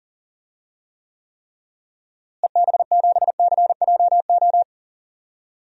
Morse code, heard as a keyed tone, about 40 wpm, 700 Hz. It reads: EB7CJO
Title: DXCC-100 Call Signs - 6 Characters - Repeated 3x - 40wpm